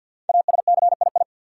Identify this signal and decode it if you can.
Morse code, heard as a keyed tone, about 46 wpm, 700 Hz. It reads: ASCII
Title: Sets of 5 Words - New Word Drill - 50wpm